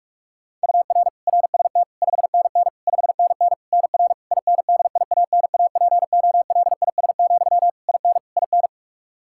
Morse code, heard as a keyed tone, about 48 wpm, 700 Hz. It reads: UR RST 5NN 5NN NR INDIANAPOLIS, IN IN
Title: K1USN SST QSO Practice 45wpm